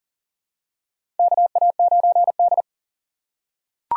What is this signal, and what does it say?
Morse code, heard as a keyed tone, about 38 wpm, 700 Hz. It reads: KA9D
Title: USA Call Signs - 4 Characters - 2x Character Spacing - 40wpm